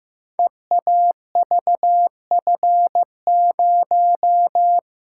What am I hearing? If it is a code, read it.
Morse code, heard as a keyed tone, about 15 wpm, 700 Hz. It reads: EAVF0
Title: Code Groups - 5 Letter:Numbers - 15wpm